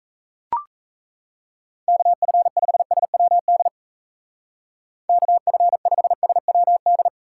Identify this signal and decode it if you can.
Morse code, heard as a keyed tone, about 36 wpm, 700 Hz. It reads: KF5SWD KF5SWD